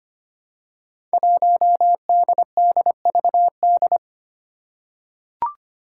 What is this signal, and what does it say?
Morse code, heard as a keyed tone, about 25 wpm, 700 Hz. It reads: 1DDVD
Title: Five Challenging Letter-Number Combinations 25wpm